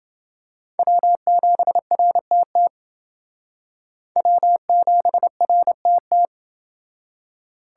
Morse code, heard as a keyed tone, about 28 wpm, 700 Hz. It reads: W7RTT W7RTT